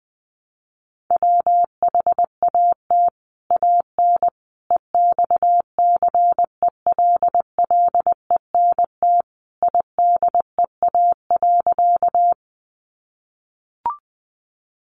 Morse code, heard as a keyed tone, about 20 wpm, 700 Hz. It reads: WHAT AN EXCELLENT IDEA.
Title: Sentences from Top 1000 Words 20wpm